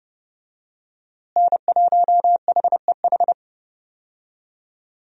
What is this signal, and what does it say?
Morse code, heard as a keyed tone, about 31 wpm, 700 Hz. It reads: N1HEH
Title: USA Call Signs - 5 Characters - Repeated 2x - 30wpm